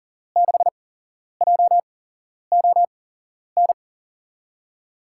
Morse code, heard as a keyed tone, about 40 wpm, spaced as for 11 wpm, 700 Hz. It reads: 6JON
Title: USA Call Signs - 6 Characters - 8x Character Spacing - 40wpm